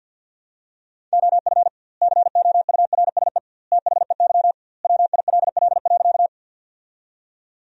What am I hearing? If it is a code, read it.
Morse code, heard as a keyed tone, about 53 wpm, 700 Hz. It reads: OF COURSE THEY WILL.
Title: 2x Word-Spacing - Sentences from Top 500 Words - 50wpm